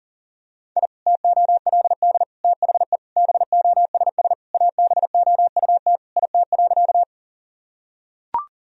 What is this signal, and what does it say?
Morse code, heard as a keyed tone, about 40 wpm, 700 Hz. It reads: I TOLD THE BOSS ABOUT IT.
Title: Sentences from Top 1000 Words - Review Part 4 - 40wpm